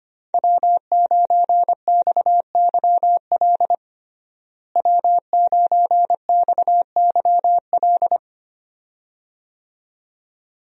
Morse code, heard as a keyed tone, about 25 wpm, 700 Hz. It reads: W9XYL W9XYL
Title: USA Call Signs - 5 Characters - Repeated 2x - 25wpm